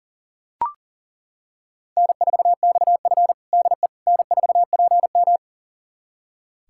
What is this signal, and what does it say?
Morse code, heard as a keyed tone, about 40 wpm, 700 Hz. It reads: N4XF DE N4PM
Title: USA Callsign DE Callsign - 4 Characters - 40wpm